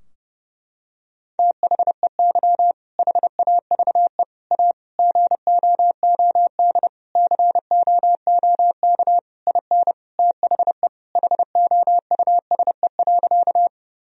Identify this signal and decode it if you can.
Morse code, heard as a keyed tone, about 30 wpm, 700 Hz. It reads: THEY HAVE A GOOD COOK IN THE HOUSE.
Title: Sentences from Top 1000 Words - Review Part 4 - 30wpm